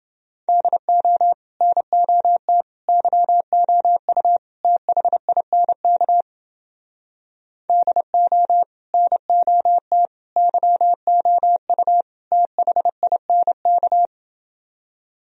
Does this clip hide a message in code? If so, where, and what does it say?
Yes: Morse code from 0.5 to 14.1 s, heard as a keyed tone, about 29 wpm, 700 Hz. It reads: DO NOT YOU THINK DO NOT YOU THINK